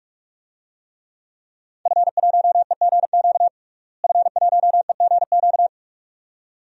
Morse code, heard as a keyed tone, about 47 wpm, 700 Hz. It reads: F1EGQ F1EGQ
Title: DXCC-100 Call Signs - 5 Characters - Repeated 2x - 45wpm